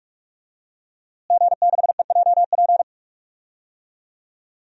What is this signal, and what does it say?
Morse code, heard as a keyed tone, about 46 wpm, 700 Hz. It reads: G6EJP
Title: DXCC-100 Call Signs - 5 Characters - Repeated 2x - 45wpm